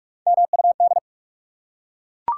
Morse code, heard as a keyed tone, about 41 wpm, 700 Hz. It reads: MUD
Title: Three Letter Words - 45wpm